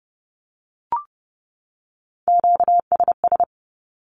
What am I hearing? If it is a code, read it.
Morse code, heard as a keyed tone, about 30 wpm, 700 Hz. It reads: QSS
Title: Q-Codes 30wpm